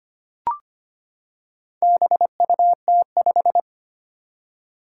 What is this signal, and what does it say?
Morse code, heard as a keyed tone, about 25 wpm, 700 Hz. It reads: BUT5
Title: Four Challenging Letter-Number Combinations 25wpm